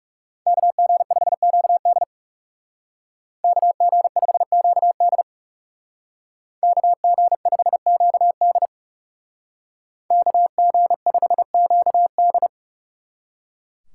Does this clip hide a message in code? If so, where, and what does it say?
Yes: Morse code from 0.5 to 12.5 s, heard as a keyed tone, about 40 wpm, 700 Hz. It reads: KG5QD KG5QD KG5QD KG5QD